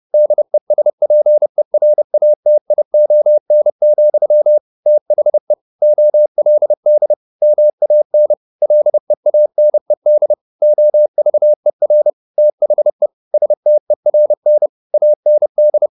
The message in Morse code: DESPERATION, THE OLD MAN LEANED OVER THE STERN AND

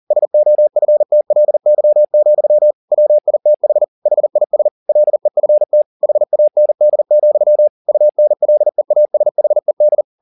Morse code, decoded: SOFTLY, WITH HIS LEFT HAND, UNLEASHED

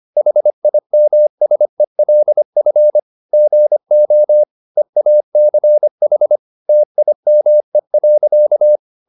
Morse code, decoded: HIMSELF GO EACH TIME.

